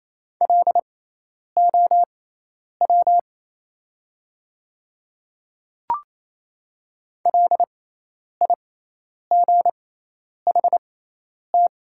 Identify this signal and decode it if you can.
Morse code, heard as a keyed tone, about 28 wpm, spaced as for 10 wpm, 700 Hz. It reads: LOW LIGHT
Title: 6x Character Spacing - Top 200 Words - 28wpm